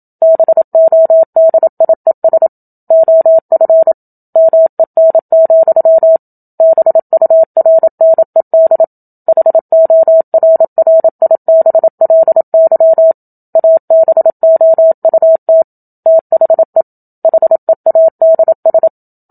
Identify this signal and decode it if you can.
Morse code, heard as a keyed tone, about 27 wpm, 650 Hz. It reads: BODIES OF MEN, BURNED HORRIBLY ABOUT THE HEADS
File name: Book Two / 01-Chapter 1 - 27 WPM